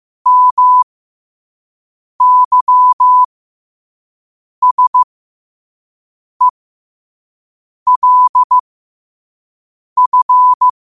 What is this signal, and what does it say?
Morse code, heard as a keyed tone, about 15 wpm, spaced as for 5 wpm, 1000 Hz. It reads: MYSELF
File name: qotdfiles / 5wpm